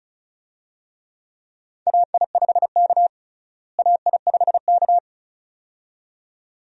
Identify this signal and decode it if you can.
Morse code, heard as a keyed tone, about 35 wpm, 700 Hz. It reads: AI5K AI5K
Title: USA Call Signs - 4 Characters - Repeated 2x - 35wpm